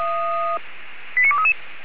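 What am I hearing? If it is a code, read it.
Five-tone selective call (ZVEI), heard as quick runs of five tones, tones of about 70 ms. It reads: 89233
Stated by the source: ZVEI five-tone selective call 89244 received by SDR.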